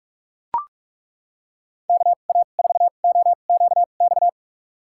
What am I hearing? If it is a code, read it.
Morse code, heard as a keyed tone, about 43 wpm, spaced as for 31 wpm, 700 Hz. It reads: KA4OQX
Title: USA Call Signs - 6 Characters - 2x Character Spacing - 45wpm